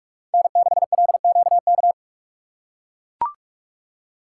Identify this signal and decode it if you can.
Morse code, heard as a keyed tone, about 44 wpm, 700 Hz. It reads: N6LQK
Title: USA Call Signs - 6x Character Spacing - 45wpm